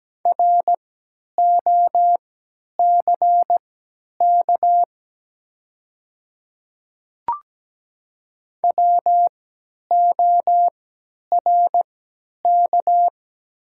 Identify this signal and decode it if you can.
Morse code, heard as a keyed tone, about 17 wpm, spaced as for 10 wpm, 700 Hz. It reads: ROCK WORK
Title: 3x Character Spacing - Top 500 Words - 17wpm